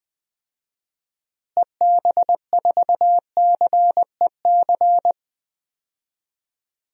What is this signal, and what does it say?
Morse code, heard as a keyed tone, about 20 wpm, 700 Hz. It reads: EB4CEC
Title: DXCC-100 Call Signs - 6 Characters - Repeated 3x - 20wpm